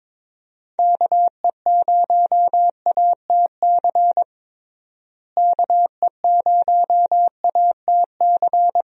Morse code, heard as a keyed tone, about 22 wpm, 700 Hz. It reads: KE0ATC KE0ATC
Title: USA Call Signs - 6 Characters - Repeated 3x - 22wpm